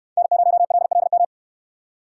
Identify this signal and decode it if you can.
Morse code, heard as a keyed tone, about 62 wpm, 700 Hz. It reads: N0LLM